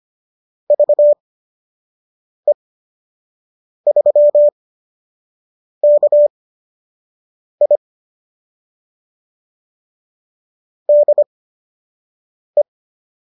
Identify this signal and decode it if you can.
Morse code, heard as a keyed tone, about 25 wpm, spaced as for 6 wpm, 590 Hz. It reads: VE3KI DE